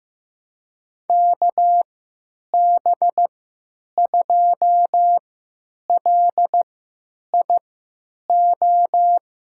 Morse code, heard as a keyed tone, about 15 wpm, spaced as for 9 wpm, 700 Hz. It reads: KB2LIO
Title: USA Call Signs - 6 Characters - 3x Character Spacing - 15wpm